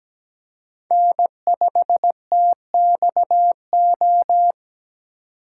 Morse code, heard as a keyed tone, about 17 wpm, 700 Hz. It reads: N5TXO
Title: USA Call Signs - 5 Characters - 2x Character Spacing - 17wpm